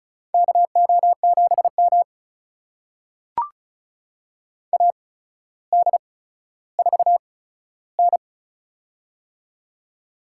Morse code, heard as a keyed tone, about 36 wpm, spaced as for 15 wpm, 700 Hz. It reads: KO7M AD4N
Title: USA Call Signs - 4 Characters - 8x Character Spacing - 35wpm